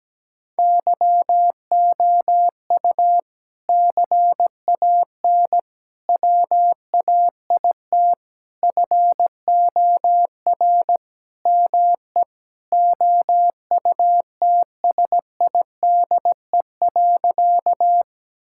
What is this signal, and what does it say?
Morse code, heard as a keyed tone, about 17 wpm, 700 Hz. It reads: YOU CAN WAIT FOR ME OUTSIDE.
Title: Sentences from Top 1000 Words - Review Part 6 - 17wpm